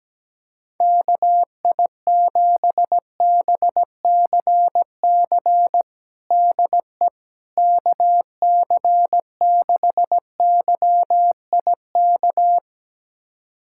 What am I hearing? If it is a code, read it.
Morse code, heard as a keyed tone, about 17 wpm, 700 Hz. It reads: KI7BCC DE KC6YIK
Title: USA Callsign DE Callsign 17wpm